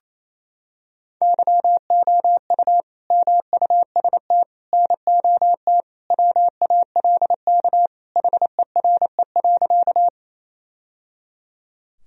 Morse code, heard as a keyed tone, about 28 wpm, 700 Hz. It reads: YOU MUST NOT WALK HERE.
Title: Sentences from Top 1000 Words - Review Part 1 - 28wpm